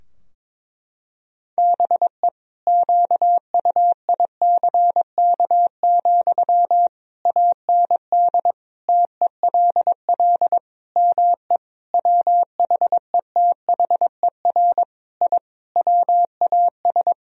Morse code, heard as a keyed tone, about 22 wpm, 700 Hz. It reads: BE QUICK, AND TELL ME WHETHER I WAS
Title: Sentences from Top 1000 Words - Review Part 3 - 22wpm